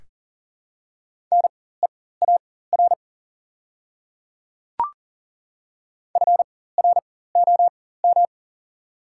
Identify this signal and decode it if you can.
Morse code, heard as a keyed tone, about 39 wpm, spaced as for 19 wpm, 700 Hz. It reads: NEAR FROM
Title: 4x Character Spacing - Top 500 Words - 40wpm